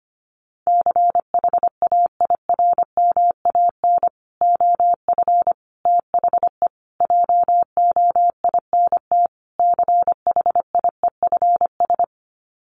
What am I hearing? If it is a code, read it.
Morse code, heard as a keyed tone, about 25 wpm, 700 Hz. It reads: CHAIRMAN OF THE JOINT CHIEFS
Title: Sets of 5 Words Encore 25wpm